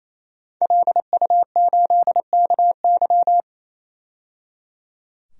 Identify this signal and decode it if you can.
Morse code, heard as a keyed tone, about 28 wpm, 700 Hz. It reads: LU8KY
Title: DXCC-100 Call Signs - 5 Characters - 28wpm